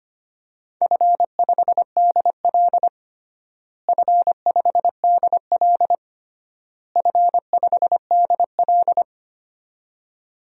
Morse code, heard as a keyed tone, about 25 wpm, 700 Hz. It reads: F5DL F5DL F5DL
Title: DXCC-100 Call Signs - 4 Characters - Repeated 3x - 25wpm